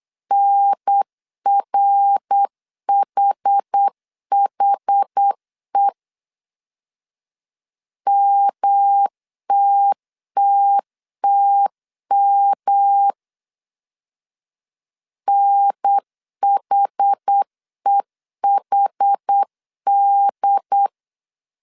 Morse code, heard as a keyed tone, about 8 wpm, 800 Hz. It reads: NRHHE MTTTM NHEHD
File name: CW / DET / 1 3 1500C S C 8wpm 4214